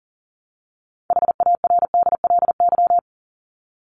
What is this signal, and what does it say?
Morse code, heard as a keyed tone, about 40 wpm, 700 Hz. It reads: HARDLY
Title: Top 1751-2000 Words 40wpm